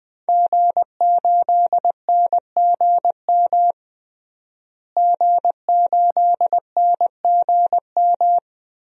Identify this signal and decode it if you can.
Morse code, heard as a keyed tone, about 20 wpm, 700 Hz. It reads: G8NGM G8NGM
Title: DXCC-100 Call Signs - 5 Characters - Repeated 3x - 20wpm